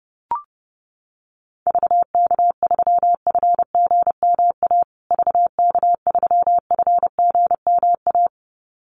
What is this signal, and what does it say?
Morse code, heard as a keyed tone, about 30 wpm, 700 Hz. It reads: VK3FGMA VK3FGMA
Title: USA Call Signs Repeated 2x 30wpm